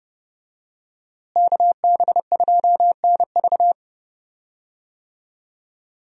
Morse code, heard as a keyed tone, about 30 wpm, 700 Hz. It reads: KB2NV
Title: USA Call Signs - 5 Characters - Repeated 2x - 30wpm